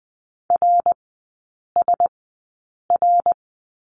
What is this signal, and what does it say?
Morse code, heard as a keyed tone, about 20 wpm, spaced as for 8 wpm, 700 Hz. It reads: RSR
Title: 053 Lesson 10e - Letter r - Three Characters 20wpm